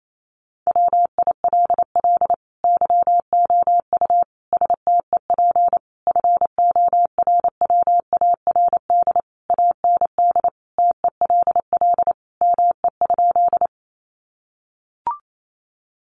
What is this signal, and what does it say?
Morse code, heard as a keyed tone, about 28 wpm, 700 Hz. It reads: WILL YOU STEP FORWARD AND TELL ME?
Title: Sentences from Top 1300 Words 28wpm